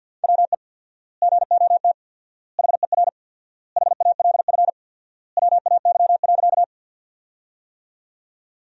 Morse code, heard as a keyed tone, about 50 wpm, 700 Hz. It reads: WE GOT HER HALF WAY.
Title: 4x Word-Spacing - Sentences from Top 400 Words - 50wpm